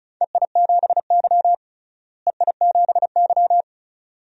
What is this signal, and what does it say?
Morse code, heard as a keyed tone, about 36 wpm, 700 Hz. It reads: EI7Y EI7Y
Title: DXCC-100 Call Signs - 4 Characters - Repeated 2x - 35wpm